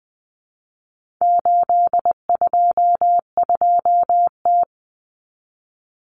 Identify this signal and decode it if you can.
Morse code, heard as a keyed tone, about 20 wpm, 700 Hz. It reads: 822T
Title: Common Cut Numbers - 4-Digit 20wpm